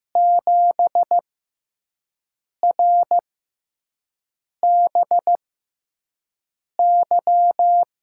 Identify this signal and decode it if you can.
Morse code, heard as a keyed tone, about 15 wpm, spaced as for 5 wpm, 700 Hz. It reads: 7RBY
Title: USA Call Signs - 6 Characters - 6x Character Spacing - 15wpm